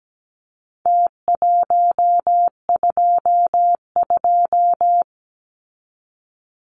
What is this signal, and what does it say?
Morse code, heard as a keyed tone, about 17 wpm, 700 Hz. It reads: T122
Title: Numbers - Contest Style - 4-Digit 17wpm